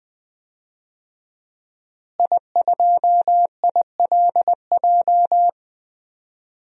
Morse code, heard as a keyed tone, about 20 wpm, 700 Hz. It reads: I2ILJ
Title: DXCC-100 Call Signs - 5 Characters - Repeated 2x - 20wpm